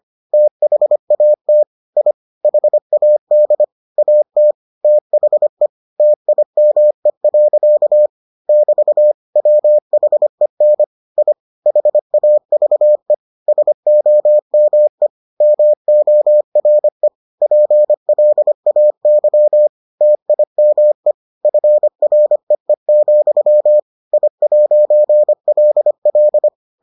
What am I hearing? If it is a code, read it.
Morse code, heard as a keyed tone, about 25 wpm, 600 Hz. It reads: THAT I HAD AT THE TIME. <BT> WHEN I HAVE SOME MORE PLAY TIME FREE, I'LL